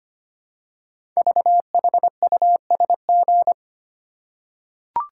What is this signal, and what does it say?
Morse code, heard as a keyed tone, about 25 wpm, 700 Hz. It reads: VHUSG
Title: Code Groups - 5 Letters - 25wpm